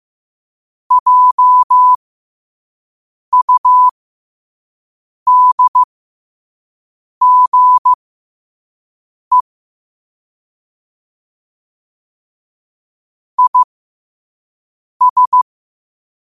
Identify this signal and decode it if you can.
Morse code, heard as a keyed tone, about 15 wpm, spaced as for 5 wpm, 1000 Hz. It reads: JUDGE IS
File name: qotdfiles / 5wpm